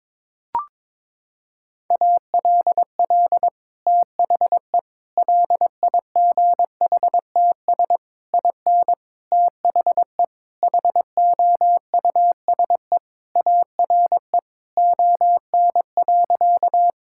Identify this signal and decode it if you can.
Morse code, heard as a keyed tone, about 22 wpm, 700 Hz. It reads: ALL THE LIGHTS IN THE HOUSE ARE ON.
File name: practice / 22wpm / Sentences with 2-Word Phrases - 22wpm